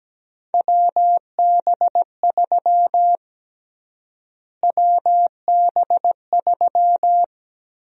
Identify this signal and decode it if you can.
Morse code, heard as a keyed tone, about 17 wpm, 700 Hz. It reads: WB3 WB3
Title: USA Call Sign Prefixes - Repeated 2x - 17wpm